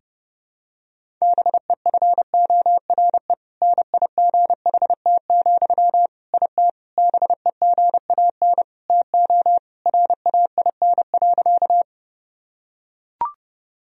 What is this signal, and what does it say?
Morse code, heard as a keyed tone, about 32 wpm, 700 Hz. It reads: BEFORE NIGHT, IT BEGAN TO RAIN.
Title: Sentences from Top 1000 Words - Review Part 1 - 30wpm